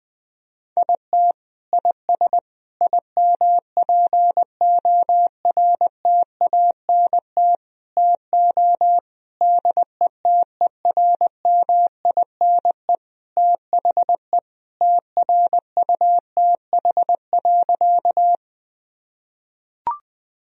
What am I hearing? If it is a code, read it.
Morse code, heard as a keyed tone, about 20 wpm, 700 Hz. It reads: IT IS IMPORTANT TO DETERMINE THE TRUTH.
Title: Sentences from Top 1000 Words - Review Part 4 - 20wpm